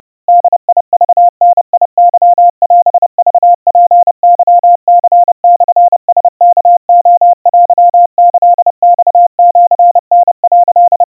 Morse code (Japanese Wabun code), heard as a keyed tone, about 30 wpm, 700 Hz. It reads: ボクダケオクツケニモラワレテキマシタ」
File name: 110 cpm 700 Hz Crispy